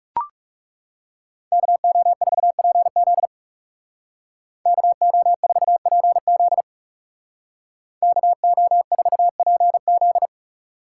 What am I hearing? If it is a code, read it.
Morse code, heard as a keyed tone, about 36 wpm, 700 Hz. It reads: KO4PZ KO4PZ KO4PZ
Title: Speed Racing - USA 2x2 Call Signs 30wpm